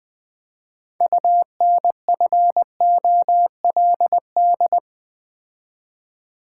Morse code, heard as a keyed tone, about 20 wpm, 700 Hz. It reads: UNFOLD
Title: 6 Letter Words - First 3 Letters Given - 20wpm